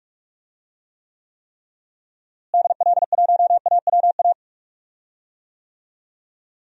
Morse code, heard as a keyed tone, about 43 wpm, 700 Hz. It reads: DL1AWA
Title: DXCC-100 Call Signs - 6 Characters - Repeated 3x - 45wpm